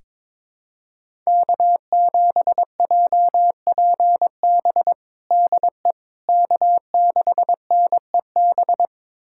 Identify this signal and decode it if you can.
Morse code, heard as a keyed tone, about 22 wpm, 700 Hz. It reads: K7JPB DE K6NEB
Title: USA Callsign DE Callsign 22wpm